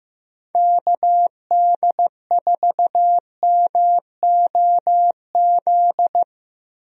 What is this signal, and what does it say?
Morse code, heard as a keyed tone, about 15 wpm, 700 Hz. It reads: KD4MOZ
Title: USA Call Signs - 6 Characters - 15wpm